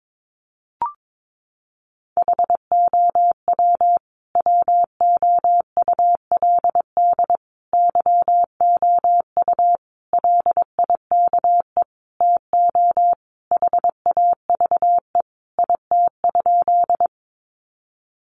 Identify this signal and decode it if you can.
Morse code, heard as a keyed tone, about 22 wpm, 700 Hz. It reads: HOW WOULD YOU LIKE TO HAVE IT?
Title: Sentences from Top 100 Words 22wpm